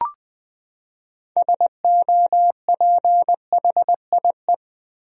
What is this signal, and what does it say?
Morse code, heard as a keyed tone, about 21 wpm, 700 Hz. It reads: SOPHIE
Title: USA Common Names - 20wpm